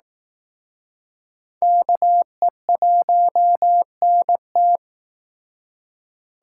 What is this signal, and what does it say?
Morse code, heard as a keyed tone, about 19 wpm, 700 Hz. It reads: KE1NT